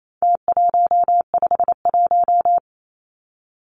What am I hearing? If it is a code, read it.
Morse code, heard as a keyed tone, about 28 wpm, 700 Hz. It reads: T151
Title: Numbers - Contest Style - 4-Digit 28wpm